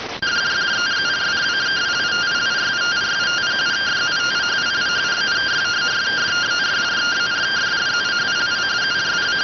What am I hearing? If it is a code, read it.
Two-tone teleprinter signal, about 170 Hz shift.